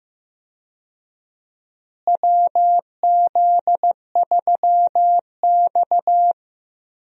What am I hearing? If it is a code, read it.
Morse code, heard as a keyed tone, about 15 wpm, 700 Hz. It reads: WZ3X
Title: USA Call Signs - 4 Characters - Repeated 2x - 15wpm